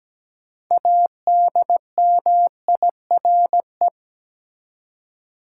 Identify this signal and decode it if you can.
Morse code, heard as a keyed tone, about 17 wpm, 700 Hz. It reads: ADMIRE
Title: Six Letter Words - 17wpm